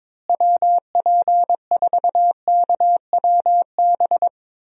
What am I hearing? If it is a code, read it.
Morse code, heard as a keyed tone, about 22 wpm, 700 Hz. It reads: WP4KWB
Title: USA Call Signs - 6 Characters - Repeated 3x - 22wpm